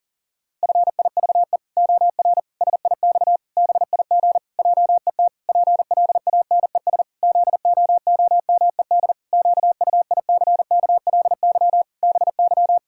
Morse code, heard as a keyed tone, about 37 wpm, 700 Hz. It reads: FIVE OR SIX BIG JET PLANES ZOOMED QUICKLY BY
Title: Pangram Challenge v2 - 40wpm